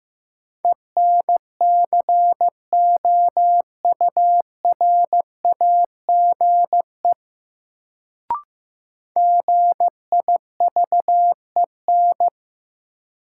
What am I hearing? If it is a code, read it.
Morse code, heard as a keyed tone, about 15 wpm, 700 Hz. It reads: ENCOURAGE GIVEN
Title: Sentences from Top 1400 Words - New Word Drill - 15wpm